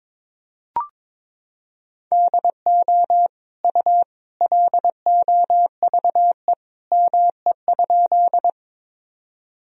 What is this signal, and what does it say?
Morse code, heard as a keyed tone, about 22 wpm, 700 Hz. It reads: DO U LOVE ME?
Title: Sentences from Top 400 Words and Abbreviations 22wpm